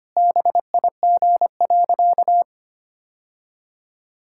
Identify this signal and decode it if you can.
Morse code, heard as a keyed tone, about 25 wpm, 700 Hz. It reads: BIG.
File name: practice / 25wpm / Sentences with 3-Word Phrases - 25wpm